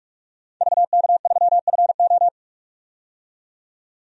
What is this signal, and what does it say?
Morse code, heard as a keyed tone, about 42 wpm, 700 Hz. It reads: VK3FO
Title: DXCC-100 Call Signs - 5 Characters - 45wpm